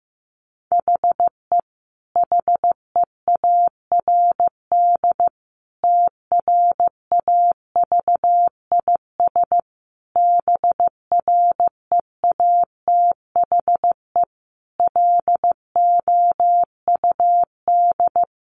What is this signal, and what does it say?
Morse code, heard as a keyed tone, about 15 wpm, 700 Hz. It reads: HE HEARD TRAVIS BREATHE LOUD
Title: A Sound Of Thunder 15wpm